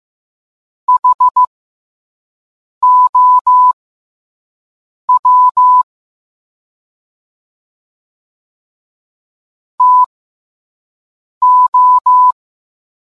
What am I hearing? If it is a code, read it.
Morse code, heard as a keyed tone, about 15 wpm, spaced as for 5 wpm, 1000 Hz. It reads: HOW TO